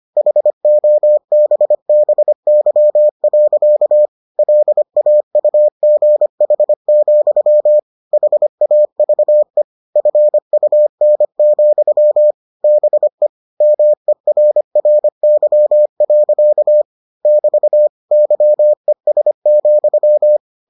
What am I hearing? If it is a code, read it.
Morse code, heard as a keyed tone, about 25 wpm, 600 Hz. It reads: HOBBY. LAUGH, HAVE FUN, BE MERRY. <BT> YES,